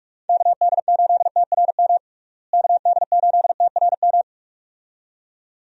Morse code, heard as a keyed tone, about 42 wpm, 700 Hz. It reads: KD8TRM KD8TRM
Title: USA Call Signs - 6 Characters - Repeated 2x - 45wpm